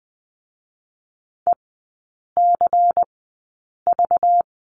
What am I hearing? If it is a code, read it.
Morse code, heard as a keyed tone, about 20 wpm, spaced as for 8 wpm, 690 Hz. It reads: ECV
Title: 205 Lesson 33e - Letter j - Three Characters 20wpm